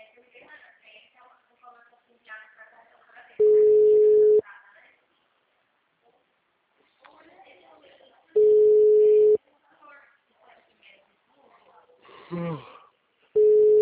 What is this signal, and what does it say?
Ringback tone (Europe, 425 Hz), with speech.
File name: out- -2007- - - .36428